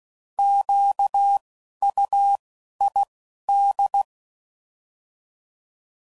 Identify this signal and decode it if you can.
Morse code, heard as a keyed tone, about 16 wpm, 800 Hz. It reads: Q U I D
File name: Prefixes / By First Letter / Qs